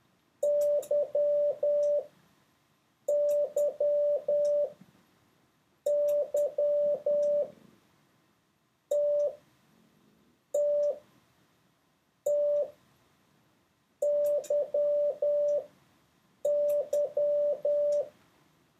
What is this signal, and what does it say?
Morse code, heard as a keyed tone, about 10 wpm, spaced as for 5 wpm, 580 Hz. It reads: YYYTTTYY
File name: 3of6-mix-and-match